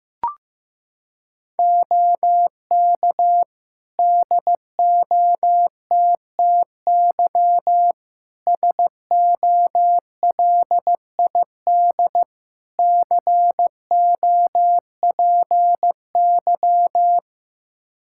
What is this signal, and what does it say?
Morse code, heard as a keyed tone, about 15 wpm, 700 Hz. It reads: OK DOTTY SOLID COPY
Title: K1USN SST QSO Practice 15wpm